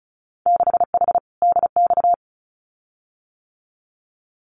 Morse code, heard as a keyed tone, about 34 wpm, 700 Hz. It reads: 6H DX
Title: Winter Field Day Exchanges Encore 35wpm